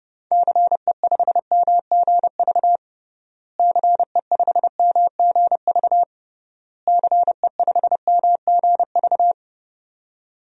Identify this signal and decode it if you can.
Morse code, heard as a keyed tone, about 32 wpm, 700 Hz. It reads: CE5MGV CE5MGV CE5MGV
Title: DXCC-100 Call Signs - 6 Characters - Repeated 3x - 30wpm